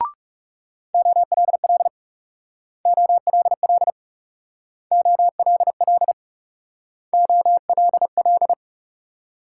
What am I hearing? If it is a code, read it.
Morse code, heard as a keyed tone, about 39 wpm, 700 Hz. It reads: OLL OLL OLL OLL